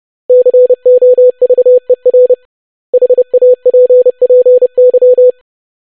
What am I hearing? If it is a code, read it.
Morse code, heard as a keyed tone, about 30 wpm, 500 Hz. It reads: COVER HAPPY
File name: basic500-25wpm